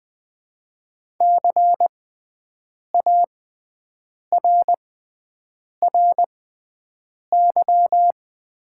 Morse code, heard as a keyed tone, about 20 wpm, spaced as for 7 wpm, 700 Hz. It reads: CARRY
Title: 6x Character Spacing - Top 300 Words - 20wpm